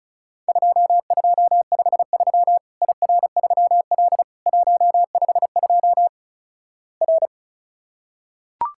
Morse code, heard as a keyed tone, about 36 wpm, 700 Hz. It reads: 2253 IR3L 152 R
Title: Worked All Europe DX Contest - QTC Exchange - 35wpm